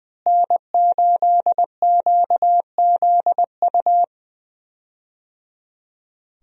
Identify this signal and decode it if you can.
Morse code, heard as a keyed tone, about 20 wpm, 700 Hz. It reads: N8QZU
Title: USA Call Signs - 5 Characters - Repeated 3x - 20wpm